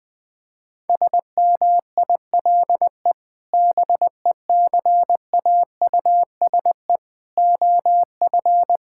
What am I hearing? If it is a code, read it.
Morse code, heard as a keyed tone, about 20 wpm, 700 Hz. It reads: SMILE BECAUSE OF
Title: Sentences with 2-Word Phrases - 20wpm